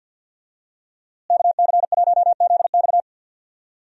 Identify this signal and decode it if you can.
Morse code, heard as a keyed tone, about 51 wpm, 700 Hz. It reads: KC1ZX
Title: USA Call Signs - 5 Characters - Repeated 2x - 50wpm